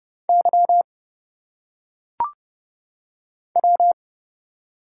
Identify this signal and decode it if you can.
Morse code, heard as a keyed tone, about 30 wpm, 700 Hz. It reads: Y W
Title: Single Letters - Phonetic Pronunciation 30wpm